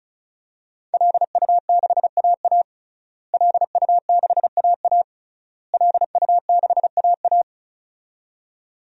Morse code, heard as a keyed tone, about 36 wpm, 700 Hz. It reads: LU6AA LU6AA LU6AA
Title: DXCC-100 Call Signs - 5 Characters - Repeated 3x - 35wpm